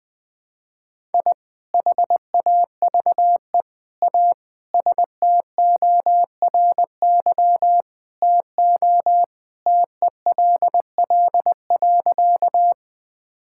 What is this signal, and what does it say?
Morse code, heard as a keyed tone, about 21 wpm, 700 Hz. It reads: I HAVE A STORY TO TELL.
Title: Sentences with 2-Word Phrases - Spoken Beginning - 20wpm